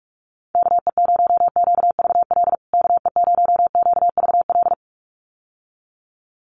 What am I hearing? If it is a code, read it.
Morse code, heard as a keyed tone, about 44 wpm, 700 Hz. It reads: KE0QVL KE0QVL
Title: USA Call Signs Repeated 2x 45wpm